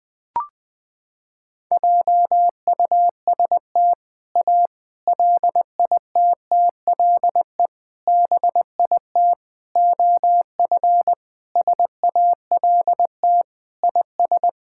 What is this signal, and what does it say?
Morse code, heard as a keyed tone, about 20 wpm, 700 Hz. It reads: JUST A LITTLE BIT OF SALT IS
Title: Sentences with 4-Word Phrases - 20wpm